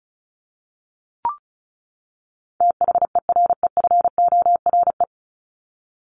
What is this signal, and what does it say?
Morse code, heard as a keyed tone, about 34 wpm, 700 Hz. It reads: THEREFORE
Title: Top 501-750 Words 35wpm